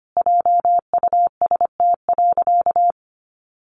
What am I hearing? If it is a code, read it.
Morse code, heard as a keyed tone, about 25 wpm, 700 Hz. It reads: JUST.
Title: Sentences from Top 1300 Words 25wpm